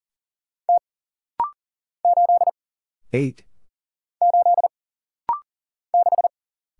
Morse code, heard as a keyed tone, about 39 wpm, 700 Hz. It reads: T 8 8 6
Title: Single Letter-Number Alt - Rapid-Fire 40wpm